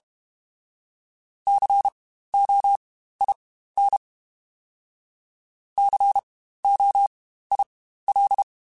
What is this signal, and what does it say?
Morse code, heard as a keyed tone, about 32 wpm, spaced as for 15 wpm, 800 Hz. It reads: COIN COIL
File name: Spaced Alphas / Spaced Alphas mp3s / Cs sp @30